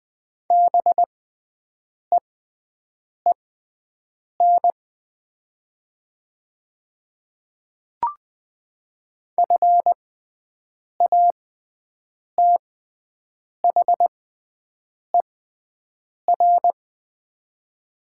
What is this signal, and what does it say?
Morse code, heard as a keyed tone, about 20 wpm, spaced as for 7 wpm, 700 Hz. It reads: BEEN FATHER
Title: 6x Character Spacing - Top 200 Words - 20wpm